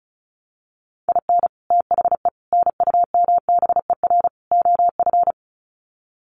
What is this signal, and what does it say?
Morse code, heard as a keyed tone, about 34 wpm, 700 Hz. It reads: IN THE NUMBER OF
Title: Sets of 4 Words 35wpm